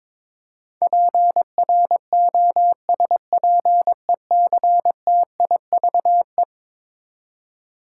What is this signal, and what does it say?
Morse code, heard as a keyed tone, about 22 wpm, 700 Hz. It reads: PROSPECTIVE
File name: practice / 22wpm / Eleven Letter Words - 22wpm